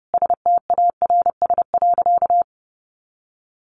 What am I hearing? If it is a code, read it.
Morse code, heard as a keyed tone, about 28 wpm, 700 Hz. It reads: STARS.
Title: Additional Easy Phrases 30wpm